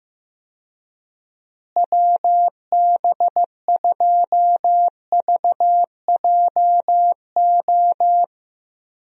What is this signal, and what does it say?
Morse code, heard as a keyed tone, about 15 wpm, 700 Hz. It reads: WB2VJO
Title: USA Call Signs - 6 Characters - Repeated 2x - 15wpm